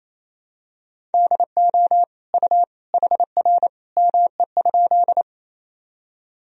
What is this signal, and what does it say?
Morse code, heard as a keyed tone, about 28 wpm, 700 Hz. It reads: DO U HR ME?
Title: Sentences from Top 300 Words and Abbreviations 28wpm